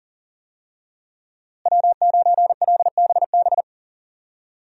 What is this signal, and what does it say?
Morse code, heard as a keyed tone, about 41 wpm, 700 Hz. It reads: W9LBB
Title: USA Call Signs - 5 Characters - Repeated 3x - 40wpm